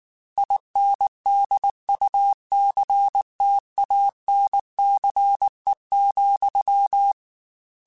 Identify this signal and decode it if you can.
Morse code, heard as a keyed tone, about 19 wpm, 800 Hz. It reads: INDUCTANCE,